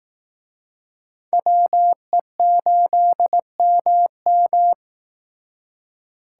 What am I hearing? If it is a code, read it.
Morse code, heard as a keyed tone, about 19 wpm, 700 Hz. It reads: WE8MM